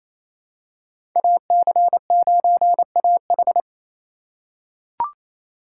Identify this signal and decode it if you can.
Morse code, heard as a keyed tone, about 28 wpm, 690 Hz. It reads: AC9AH
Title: USA Call Signs - 2x Character Spacing - 28wpm